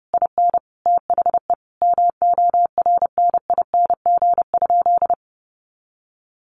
Morse code, heard as a keyed tone, about 30 wpm, 700 Hz. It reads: IN THE MORNING?
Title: Sentences from Top 1300 Words 30wpm